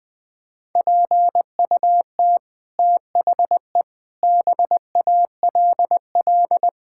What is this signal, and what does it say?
Morse code, heard as a keyed tone, about 20 wpm, 700 Hz. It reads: PUT THE BALL
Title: Sets of 9 Words - 20wpm